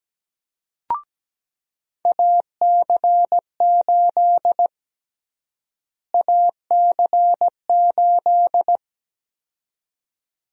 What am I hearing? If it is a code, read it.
Morse code, heard as a keyed tone, about 17 wpm, 700 Hz. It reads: AC8 AC8
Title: USA Call Sign Prefixes - Repeated 2x - 17wpm